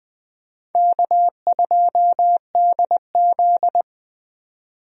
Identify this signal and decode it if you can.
Morse code, heard as a keyed tone, about 20 wpm, 700 Hz. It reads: K2DZ
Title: USA Call Signs - 4 Characters - 20wpm